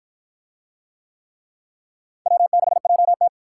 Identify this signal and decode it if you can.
Morse code, heard as a keyed tone, about 49 wpm, 700 Hz. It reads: W6JT